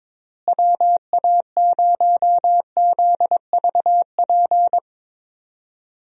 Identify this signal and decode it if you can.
Morse code, heard as a keyed tone, about 22 wpm, 700 Hz. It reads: WA0ZVP
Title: USA Call Signs - 6 Characters - Repeated 2x - 22wpm